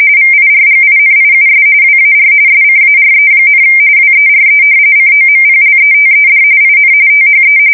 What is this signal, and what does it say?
RTTY, about 45.45 baud. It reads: VIRGINIUS WAR SCARE, WERE COMPLETED IN THE
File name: fsk170 / usnavy3